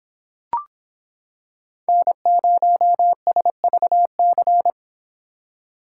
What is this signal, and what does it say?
Morse code, heard as a keyed tone, about 25 wpm, 700 Hz. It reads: N0SVC